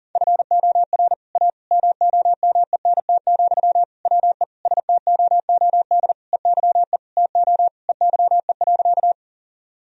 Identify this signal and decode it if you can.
Morse code, heard as a keyed tone, about 40 wpm, 700 Hz. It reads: FOR A MOMENT, WE STOOD EYE TO EYE.
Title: Sentences from Top 1000 Words - Review Part 2 - 40wpm